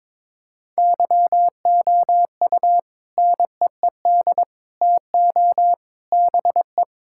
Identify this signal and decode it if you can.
Morse code, heard as a keyed tone, about 22 wpm, 700 Hz. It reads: YOU NEED TO BE
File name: practice / 22wpm / Sentences with 3-Word Phrases - 22wpm